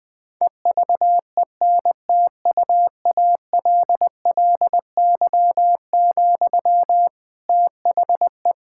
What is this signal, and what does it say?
Morse code, heard as a keyed tone, about 20 wpm, 700 Hz. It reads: EVENTUALLY, THE
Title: Sentences from Top 1000 Words - Review Part 6 - 20wpm